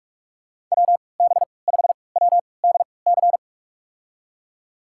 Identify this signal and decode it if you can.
Morse code, heard as a keyed tone, about 44 wpm, spaced as for 25 wpm, 690 Hz. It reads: WB5WDC
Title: USA Call Signs - 6 Characters - 3x Character Spacing - 45wpm